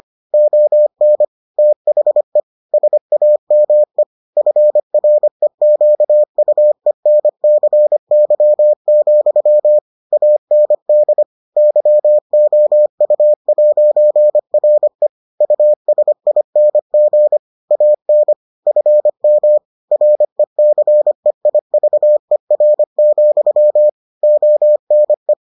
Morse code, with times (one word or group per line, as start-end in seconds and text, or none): 0.3-1.2 ON
1.6-2.4 THE
2.7-4.0 SAME
4.4-9.8 FREQUENCY,
10.1-11.2 AND
11.6-15.1 YOU'RE
15.4-17.4 USING
17.7-18.3 AN
18.7-19.6 FM
19.9-23.9 RECEIVER,
24.2-25.3 ONE